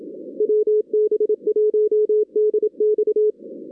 Morse code, heard as a keyed tone, about 27 wpm, 440 Hz. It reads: WB1DX
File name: 2016 arrl dx cw / 2016-ARRL-DX-CW-K5TR- - 0-41